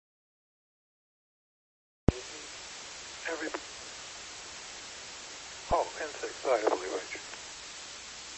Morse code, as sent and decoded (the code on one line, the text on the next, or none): none